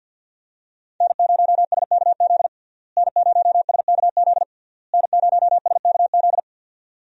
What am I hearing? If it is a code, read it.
Morse code, heard as a keyed tone, about 52 wpm, 700 Hz. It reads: N0SKZ N0SKZ N0SKZ
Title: USA Call Signs - 5 Characters - Repeated 3x - 50wpm